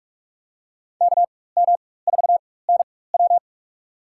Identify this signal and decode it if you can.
Morse code, heard as a keyed tone, about 43 wpm, spaced as for 21 wpm, 700 Hz. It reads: KM4NW
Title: USA Call Signs - 4x Character Spacing - 45wpm